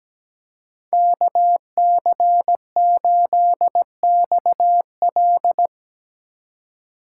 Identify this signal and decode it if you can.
Morse code, heard as a keyed tone, about 17 wpm, 700 Hz. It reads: KC8XL
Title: Code Groups - 5 Letter:Numbers - 17wpm